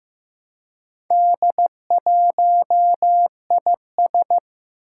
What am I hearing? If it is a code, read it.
Morse code, heard as a keyed tone, about 15 wpm, 700 Hz. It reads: D1IS
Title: Four Challenging Letter-Number Combinations 15wpm